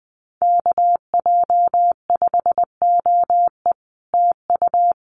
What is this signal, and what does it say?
Morse code, heard as a keyed tone, about 20 wpm, 700 Hz. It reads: KJ5OE TU
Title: Common QSO Phrases Encore 20wpm